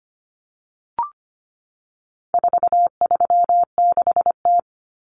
Morse code, heard as a keyed tone, about 25 wpm, 700 Hz. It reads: 436T
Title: Common Cut Numbers - 4-Digit 25wpm